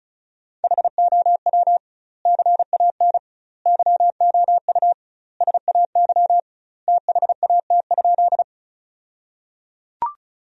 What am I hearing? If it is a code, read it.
Morse code, heard as a keyed tone, about 34 wpm, 700 Hz. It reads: HOW CAN YOU SAY THAT?
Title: 2x Word-Spacing - Sentences from Top 200 Words - 35wpm